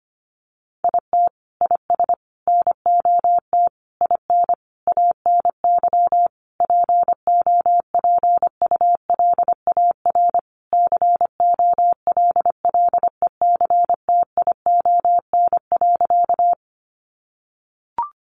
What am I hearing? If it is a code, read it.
Morse code, heard as a keyed tone, about 25 wpm, 700 Hz. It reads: IT IS NOT IN ANY POPULAR COLLECTION.
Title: Sentences from Top 1200 Words 25wpm